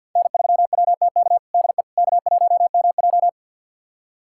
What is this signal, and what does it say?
Morse code, heard as a keyed tone, about 52 wpm, 700 Hz. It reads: N3WTK DE K1MJ
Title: USA Callsign DE Callsign 50wpm